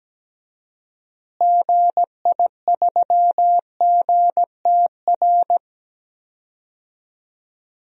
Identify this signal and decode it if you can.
Morse code, heard as a keyed tone, about 17 wpm, 700 Hz. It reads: GI3GTR
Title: DXCC-100 Call Signs - 6 Characters - Repeated 2x - 17wpm